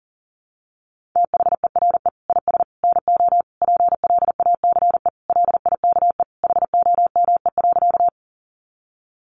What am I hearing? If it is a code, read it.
Morse code, heard as a keyed tone, about 40 wpm, 700 Hz. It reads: THERE IS NO PLACE LIKE HOME.
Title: Sentences from Top 200 Words 40wpm